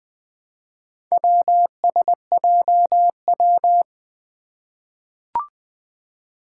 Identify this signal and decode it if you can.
Morse code, heard as a keyed tone, about 20 wpm, 700 Hz. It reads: WSJW
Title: Four Challenging Letter-Number Combinations 20wpm